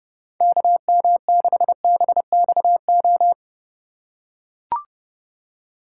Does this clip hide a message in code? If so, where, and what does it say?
Yes: Morse code from 0.4 to 3.3 s, heard as a keyed tone, about 29 wpm, 700 Hz. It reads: KM6BXO